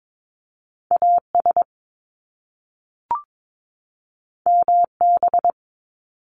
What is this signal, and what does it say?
Morse code, heard as a keyed tone, about 22 wpm, 700 Hz. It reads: AS MB
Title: Top Two Letter Combinations in Words 22wpm